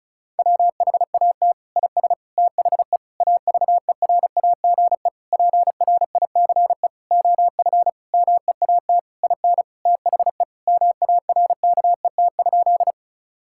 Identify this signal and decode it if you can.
Morse code, heard as a keyed tone, about 36 wpm, 700 Hz. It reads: WHAT IS THE AVERAGE PRICE OF MEAT IN THE MARKET?
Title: Sentences from Top 1000 Words - Review Part 6 - 35wpm